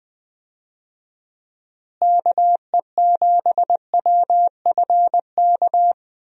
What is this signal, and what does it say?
Morse code, heard as a keyed tone, about 20 wpm, 700 Hz. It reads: KE7WFK
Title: USA Call Signs - 6 Characters - Repeated 2x - 20wpm